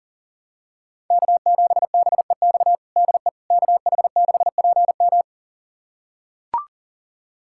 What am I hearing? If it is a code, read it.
Morse code, heard as a keyed tone, about 37 wpm, 700 Hz. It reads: K7BEX DE KH6PM
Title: USA Callsign DE Callsign - 5 Characters - 40wpm